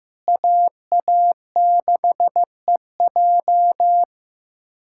Morse code, heard as a keyed tone, about 16 wpm, 700 Hz. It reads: AA6EJ
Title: USA Callsign DE Callsign - 5 Characters - 15wpm